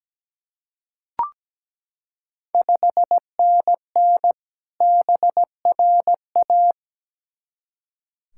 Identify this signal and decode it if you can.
Morse code, heard as a keyed tone, about 17 wpm, 690 Hz. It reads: 5NN BRA
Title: Ontario QSO Party Exchange 17wpm